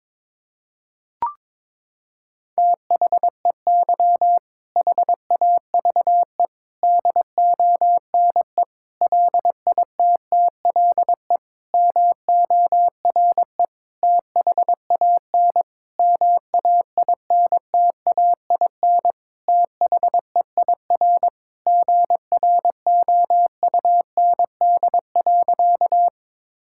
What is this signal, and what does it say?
Morse code, heard as a keyed tone, about 22 wpm, 700 Hz. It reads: THEY HAVE DONE LITTLE MORE THAN MAINTAIN THEIR GROUND.
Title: Sentences from Top 1000 Words - Review Part 5 - 22wpm